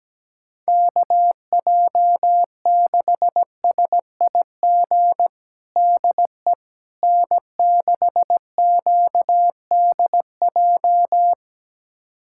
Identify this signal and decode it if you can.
Morse code, heard as a keyed tone, about 17 wpm, 700 Hz. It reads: KJ6SIG DE N6QDJ
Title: USA Callsign DE Callsign 17wpm